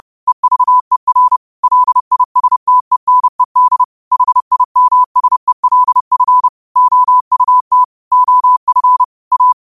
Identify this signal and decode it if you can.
Morse code, heard as a keyed tone, about 30 wpm, 1000 Hz. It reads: EVER LISTENED HIMSELF OUT OF A
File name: qotdfiles / 30wpm